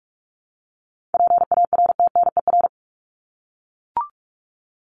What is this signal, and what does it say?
Morse code, heard as a keyed tone, about 44 wpm, 700 Hz. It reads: PARTNER
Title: Top 1501-1750 Words 45wpm